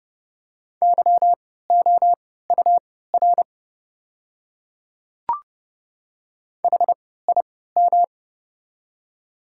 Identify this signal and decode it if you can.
Morse code, heard as a keyed tone, about 30 wpm, spaced as for 17 wpm, 700 Hz. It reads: YOUR HIM
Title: 3x Character Spacing - Top 100 Words - 30wpm